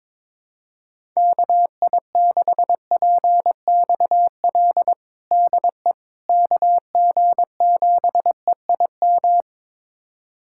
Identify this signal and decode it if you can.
Morse code, heard as a keyed tone, about 22 wpm, 700 Hz. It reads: KI6PXL DE KG7EIM
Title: USA Callsign DE Callsign 22wpm